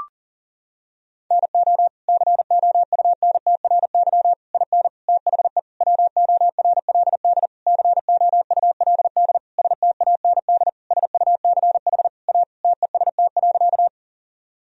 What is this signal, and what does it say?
Morse code, heard as a keyed tone, about 40 wpm, 700 Hz. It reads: NO COUNTRY IN THE WORLD COULD STAND SUCH A TEST.
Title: Sentences from Top 1000 Words - Review Part 3 - 40wpm